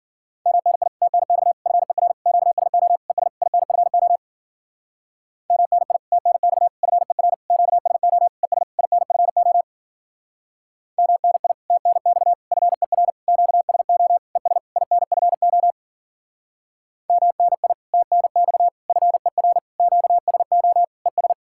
Morse code, heard as a keyed tone, about 50 wpm, 700 Hz. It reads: MNI TNX FER QSO ES INFO MNI TNX FER QSO ES INFO MNI TNX FER QSO ES INFO MNI TNX FER QSO ES
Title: Speed-Racing Common QSO Phrases - 40wpm